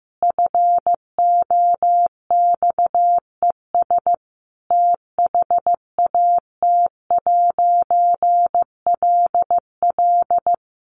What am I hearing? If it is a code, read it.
Morse code, heard as a keyed tone, about 15 wpm, 700 Hz. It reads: FOXES THAT'LL
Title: A Sound Of Thunder 15wpm